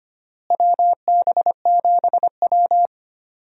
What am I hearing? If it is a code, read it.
Morse code, heard as a keyed tone, about 25 wpm, 700 Hz. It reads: WB7W
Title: USA Call Signs - 4 Characters - Repeated 3x - 25wpm